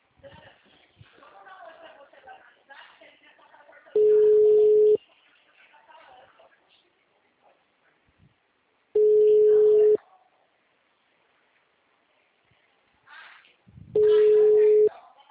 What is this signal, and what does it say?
Ringback tone (Europe, 425 Hz), with speech.
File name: out- -2033- - - .63